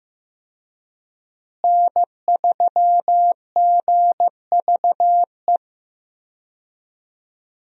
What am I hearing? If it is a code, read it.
Morse code, heard as a keyed tone, about 15 wpm, 700 Hz. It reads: N3GVE
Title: USA Call Signs - 5 Characters - Repeated 3x - 15wpm